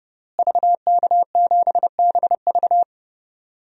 Morse code, heard as a keyed tone, about 30 wpm, 700 Hz. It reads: VK7BV
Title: DXCC-100 Call Signs - 5 Characters - Repeated 2x - 30wpm